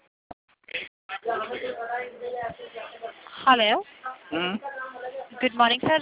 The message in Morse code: none